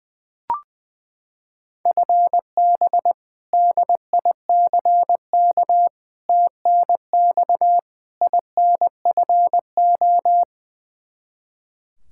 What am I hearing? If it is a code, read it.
Morse code, heard as a keyed tone, about 20 wpm, 700 Hz. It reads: FB DICK TNX INFO
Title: Practice QSO - Exchange 3 - More Info - 20wpm